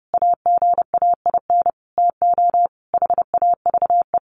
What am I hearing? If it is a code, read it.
Morse code, heard as a keyed tone, about 30 wpm, 700 Hz. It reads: AGAIN TO HAVE
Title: Sentences from Top 1500 Words 30wpm